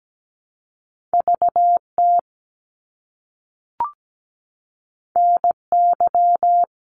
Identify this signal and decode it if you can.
Morse code, heard as a keyed tone, about 17 wpm, 700 Hz. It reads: VT NY
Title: US States Repeated 3x 17wpm